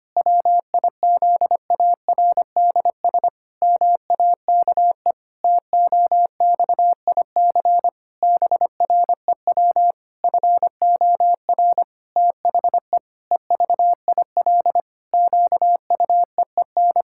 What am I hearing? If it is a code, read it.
Morse code, heard as a keyed tone, about 25 wpm, 700 Hz. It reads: WIZARDS MAKE TOXIC BREW FOR THE EVIL QUEEN
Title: Pangram Challenge v2 - 25wpm